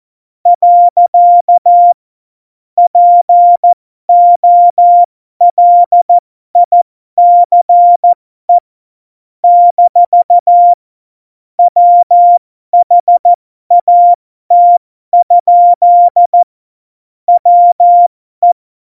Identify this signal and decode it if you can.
Morse code, heard as a keyed tone, about 14 wpm, spaced as for 12 wpm, 700 Hz. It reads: . POLICE - WHAT? WE
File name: Morsekurs / Witze